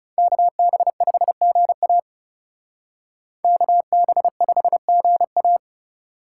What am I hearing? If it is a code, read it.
Morse code, heard as a keyed tone, about 36 wpm, 700 Hz. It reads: KB5GA KB5GA
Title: Speed Racing - USA Call Signs 30wpm